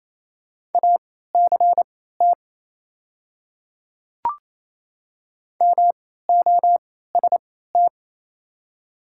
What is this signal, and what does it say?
Morse code, heard as a keyed tone, about 28 wpm, spaced as for 16 wpm, 700 Hz. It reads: ACT MOST
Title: 3x Character Spacing - Top 300 Words - 28wpm